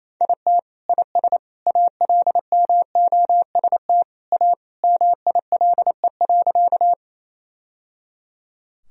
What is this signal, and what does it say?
Morse code, heard as a keyed tone, about 28 wpm, 700 Hz. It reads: IT IS ALMOST A MILE.
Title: Sentences from Top 1000 Words - Review Part 4 - 28wpm